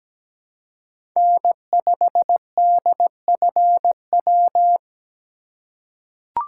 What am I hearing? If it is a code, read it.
Morse code, heard as a keyed tone, about 17 wpm, 700 Hz. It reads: N5DFW
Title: USA Call Signs - 5 Characters - 4x Character Spacing - 17wpm